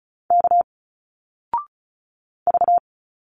Morse code, heard as a keyed tone, about 36 wpm, 700 Hz. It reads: K V
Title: Single Letters 35wpm